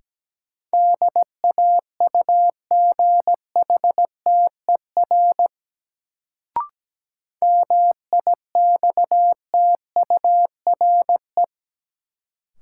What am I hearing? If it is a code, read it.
Morse code, heard as a keyed tone, about 17 wpm, 700 Hz. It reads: DAUGHTER MIXTURE
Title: Sentences from Top 1500 Words - New Word Drill Alt - 17wpm